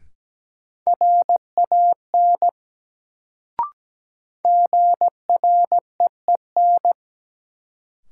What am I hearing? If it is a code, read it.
Morse code, heard as a keyed tone, about 17 wpm, 700 Hz. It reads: RAN GREEN
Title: Sentences from Top 500 Words - New Word Drill - 17wpm